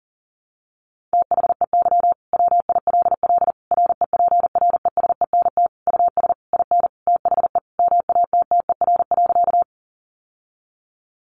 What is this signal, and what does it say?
Morse code, heard as a keyed tone, about 40 wpm, 700 Hz. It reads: THEY WILL REPRESENT US IN THE MATTER.
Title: Sentences from Top 1200 Words 40wpm